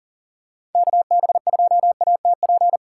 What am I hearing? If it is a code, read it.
Morse code, heard as a keyed tone, about 40 wpm, 700 Hz. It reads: KB2ATP
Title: USA Call Signs - 6 Characters - 2x Character Spacing - 40wpm